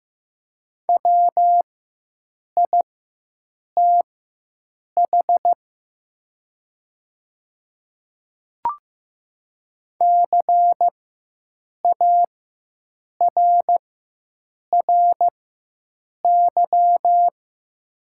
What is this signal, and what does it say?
Morse code, heard as a keyed tone, about 15 wpm, spaced as for 7 wpm, 700 Hz. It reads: WITH CARRY
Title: 4x Character Spacing - Top 300 Words - 15wpm